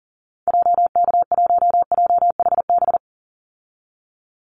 Morse code, heard as a keyed tone, about 39 wpm, 700 Hz. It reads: JK1JHB
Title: Call Signs Repeated 2x - 40wpm